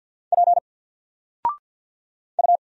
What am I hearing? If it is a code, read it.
Morse code, heard as a keyed tone, about 49 wpm, 700 Hz. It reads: P U
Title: Single Letter-Number 50wpm